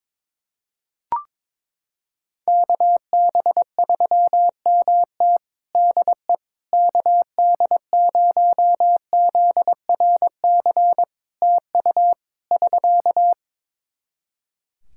Morse code, heard as a keyed tone, about 21 wpm, 700 Hz. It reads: KB3MT DE KD0ZRC TU <SK>
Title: Practice QSO - Exchange 4 - Ending - 22wpm